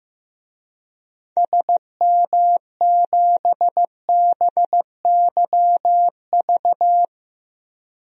Morse code, heard as a keyed tone, about 15 wpm, 700 Hz. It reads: SM7BYV
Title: DXCC-100 Call Signs - 6 Characters - Repeated 3x - 15wpm